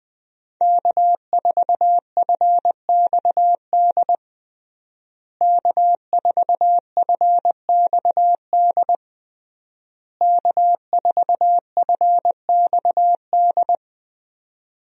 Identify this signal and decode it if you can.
Morse code, heard as a keyed tone, about 20 wpm, 700 Hz. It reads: K4FXD K4FXD K4FXD
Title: USA Call Signs - 5 Characters - Repeated 3x - 20wpm